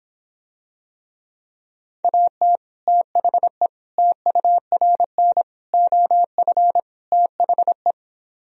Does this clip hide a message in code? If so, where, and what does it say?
Yes: Morse code from 2.0 to 7.9 s, heard as a keyed tone, about 26 wpm, 700 Hz. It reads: AT THE TURN OF THE